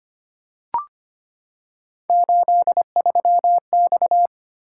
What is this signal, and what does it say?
Morse code, heard as a keyed tone, about 25 wpm, 700 Hz. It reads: 83X
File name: Three Letter-Number Combinations - 25wpm